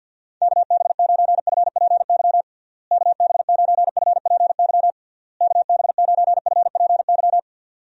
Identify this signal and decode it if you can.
Morse code, heard as a keyed tone, about 52 wpm, 700 Hz. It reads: KB9FPY KB9FPY KB9FPY
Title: USA Call Signs - 6 Characters - Repeated 3x - 50wpm